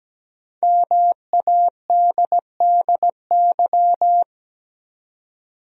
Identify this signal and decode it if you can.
Morse code, heard as a keyed tone, about 17 wpm, 700 Hz. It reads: MADDY
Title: USA Common Names - 17wpm